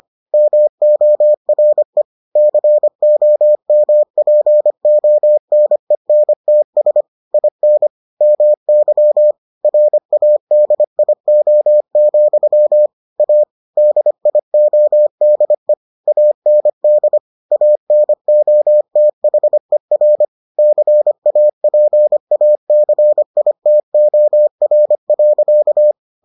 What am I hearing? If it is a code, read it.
Morse code, heard as a keyed tone, about 25 wpm, 600 Hz. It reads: MORE COMPONENTS IN MY RADIO, A DIODE AND ANOTHER CAPACITOR.